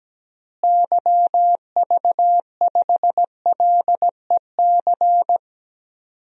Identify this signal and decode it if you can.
Morse code, heard as a keyed tone, about 17 wpm, 700 Hz. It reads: YV5LEC
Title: DXCC-100 Call Signs - 6 Characters - Repeated 3x - 17wpm